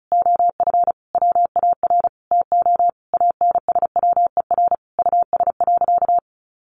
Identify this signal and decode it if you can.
Morse code, heard as a keyed tone, about 34 wpm, 700 Hz. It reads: OF WAR TO ANSWER US.
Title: Sentences from Top 1500 Words 35wpm